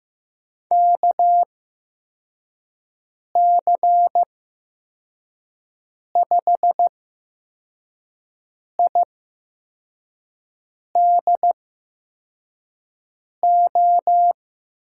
Morse code, heard as a keyed tone, about 15 wpm, spaced as for 4 wpm, 700 Hz. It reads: KC5IDO
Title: USA Call Signs - 6 Characters - 8x Character Spacing - 15wpm